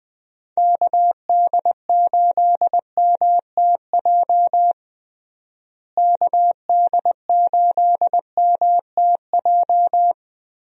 Morse code, heard as a keyed tone, about 20 wpm, 700 Hz. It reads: KD8MTJ KD8MTJ
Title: USA Call Signs - 6 Characters - Repeated 3x - 20wpm